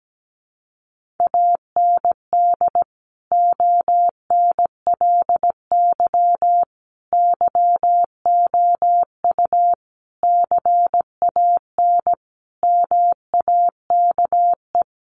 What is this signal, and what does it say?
Morse code, heard as a keyed tone, about 17 wpm, 700 Hz. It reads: AND ONLY YOU CAN MAKE
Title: Sentences from Top 1400 Words 17wpm